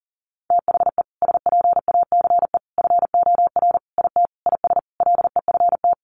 Morse code, heard as a keyed tone, about 40 wpm, 700 Hz. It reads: THE SPACE FOR IT IS LEFT
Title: Sentences from Top 1500 Words 40wpm